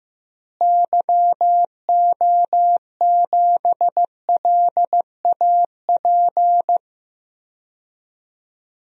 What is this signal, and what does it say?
Morse code, heard as a keyed tone, about 15 wpm, 700 Hz. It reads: YO7LAP
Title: DXCC-100 Call Signs - 6 Characters - Repeated 3x - 15wpm